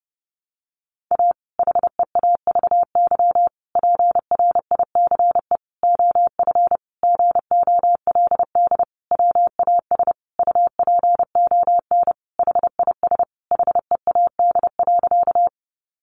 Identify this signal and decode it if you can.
Morse code, heard as a keyed tone, about 30 wpm, 700 Hz. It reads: A HEAVY PRICE OF GOLD WAS UPON HIS HEAD.
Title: Sentences from Top 1100 Words 30wpm